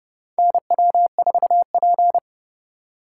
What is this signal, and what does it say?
Morse code, heard as a keyed tone, about 30 wpm, 700 Hz. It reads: NW4P
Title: USA Call Signs - 4 Characters - 4x Character Spacing - 30wpm